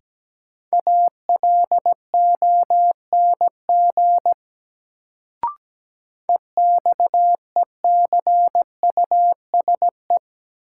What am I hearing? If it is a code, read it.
Morse code, heard as a keyed tone, about 17 wpm, 700 Hz. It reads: ALONG EXECUSE
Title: Sentences from Top 700 Words - New Word Drill - 17wpm